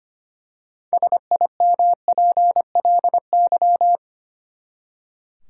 Morse code, heard as a keyed tone, about 25 wpm, 700 Hz. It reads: SIMPLY
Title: Six Letter Words - 25wpm